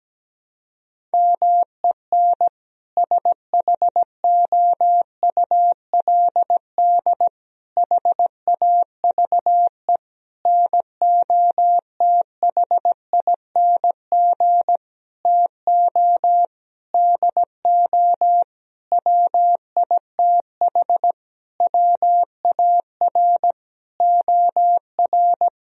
Morse code, heard as a keyed tone, about 17 wpm, 700 Hz. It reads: MEN SHOULD HAVE NOTHING TO DO WITH WAR OR
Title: Sentences from Top 1000 Words - Review Part 3 - 17wpm